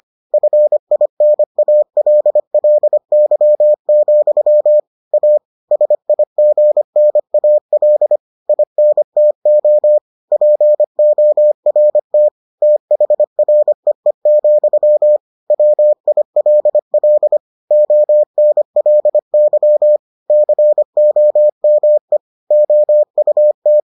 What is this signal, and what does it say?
Morse code, heard as a keyed tone, about 25 wpm, 600 Hz. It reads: FINALLY, A SIGNAL INTO PORT THREE, WILL ONLY COME OUT